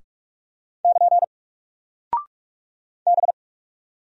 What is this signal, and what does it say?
Morse code, heard as a keyed tone, about 46 wpm, 700 Hz. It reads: <KN> B
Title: Single Character (Letter:Number:Punctuation:Prosign) Alt 45wpm